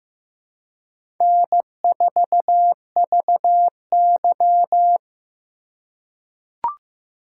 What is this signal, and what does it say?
Morse code, heard as a keyed tone, about 15 wpm, 700 Hz. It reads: N4VY
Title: USA Call Signs - 4 Characters - 2x Character Spacing - 15wpm